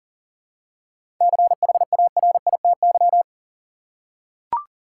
Morse code, heard as a keyed tone, about 40 wpm, 700 Hz. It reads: CHARITY
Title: Seven Letter Words - 40wpm